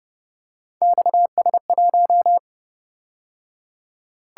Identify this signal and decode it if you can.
Morse code, heard as a keyed tone, about 29 wpm, 700 Hz. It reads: XS1
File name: Speed-Racing - Three Letter-Number Combinations - 25wpm